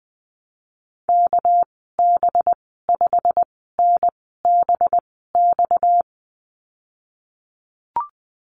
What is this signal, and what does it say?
Morse code, heard as a keyed tone, about 20 wpm, spaced as for 15 wpm, 700 Hz. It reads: KB5NBX
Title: 238 Lesson 38g - Letter q - Call Signs 20wpm